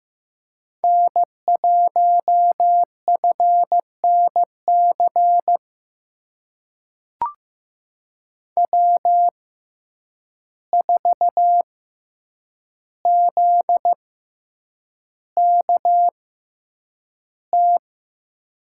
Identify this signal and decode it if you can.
Morse code, heard as a keyed tone, about 15 wpm, spaced as for 8 wpm, 700 Hz. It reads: N1FNC W4ZKT
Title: USA Call Signs - 5 Characters - 6x Character Spacing - 15wpm